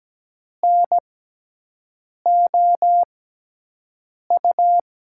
Morse code, heard as a keyed tone, about 17 wpm, spaced as for 6 wpm, 700 Hz. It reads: NOU
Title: USA Call Signs - 6 Characters - 6x Character Spacing - 17wpm